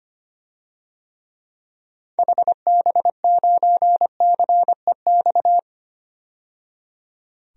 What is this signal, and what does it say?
Morse code, heard as a keyed tone, about 25 wpm, 700 Hz. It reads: HB9CEX
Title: DXCC-100 Call Signs - 6 Characters - 25wpm